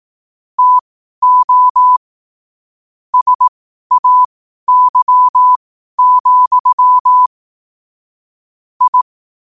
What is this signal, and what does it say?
Morse code, heard as a keyed tone, about 18 wpm, spaced as for 12 wpm, 1000 Hz. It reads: TO SAY, I
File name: qotdfiles / 12wpm